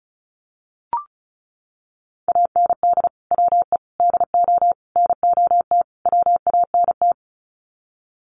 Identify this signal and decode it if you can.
Morse code, heard as a keyed tone, about 34 wpm, 700 Hz. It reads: AND WE DO NOT WANT
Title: Sets of 5 Words Encore2 35wpm